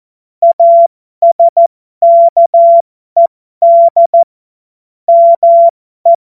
Morse code, heard as a keyed tone, about 14 wpm, spaced as for 12 wpm, 680 Hz. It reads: ASKED ME
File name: Morsekurs / Witze